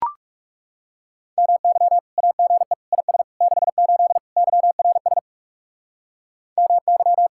Morse code, heard as a keyed tone, about 37 wpm, 700 Hz. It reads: MY AGE IS 68 YRS MY
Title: Speed-Racing Common QSO Phrases - 30wpm